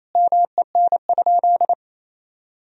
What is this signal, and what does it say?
Morse code, heard as a keyed tone, about 28 wpm, 700 Hz. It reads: MEN?
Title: Sentences from Top 1000 Words - Review Part 2 - 28wpm